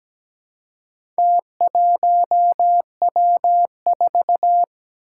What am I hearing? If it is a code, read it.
Morse code, heard as a keyed tone, about 17 wpm, 700 Hz. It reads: T1W4
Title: Four Challenging Letter-Number Combinations 17wpm